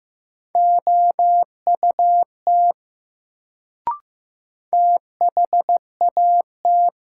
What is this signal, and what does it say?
Morse code, heard as a keyed tone, about 15 wpm, 700 Hz. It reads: OUT THAT
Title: Sentences from Top 100 Words - New Word Drill - 15wpm